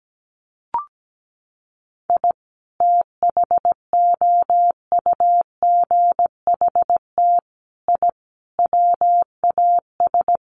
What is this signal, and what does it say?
Morse code, heard as a keyed tone, about 17 wpm, 700 Hz. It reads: I THOUGHT I WAS
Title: Sets of 5 Words Encore2 17wpm